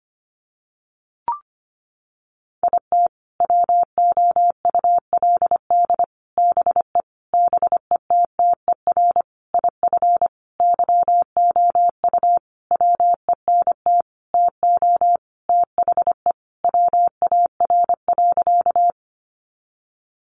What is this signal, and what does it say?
Morse code, heard as a keyed tone, about 25 wpm, 700 Hz. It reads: IT WOULD BE BETTER IF YOU WENT TO THE WAR.
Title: Sentences from Top 400 Words 25wpm